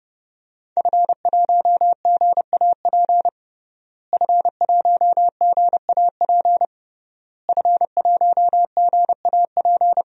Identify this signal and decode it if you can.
Morse code, heard as a keyed tone, about 30 wpm, 700 Hz. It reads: F1GAP F1GAP F1GAP
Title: DXCC-100 Call Signs - 5 Characters - Repeated 3x - 30wpm